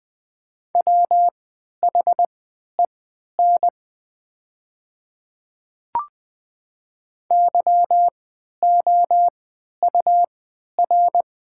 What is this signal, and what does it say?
Morse code, heard as a keyed tone, about 20 wpm, spaced as for 11 wpm, 700 Hz. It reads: WHEN YOUR
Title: 3x Character Spacing - Top 100 Words - 20wpm